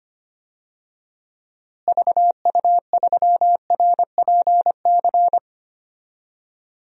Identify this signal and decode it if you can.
Morse code, heard as a keyed tone, about 25 wpm, 700 Hz. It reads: VU3RPC
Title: DXCC-100 Call Signs - 6 Characters - Repeated 2x - 25wpm